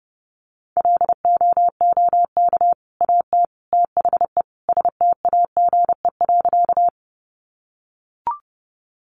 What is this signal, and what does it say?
Morse code, heard as a keyed tone, about 30 wpm, 700 Hz. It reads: LOOK AT THE STAGE.
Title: Sentences from Top 1200 Words 30wpm